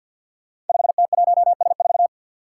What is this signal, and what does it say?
Morse code, heard as a keyed tone, about 50 wpm, 700 Hz. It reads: 5T1S4
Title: Five Challenging Letter-Number Combinations 50wpm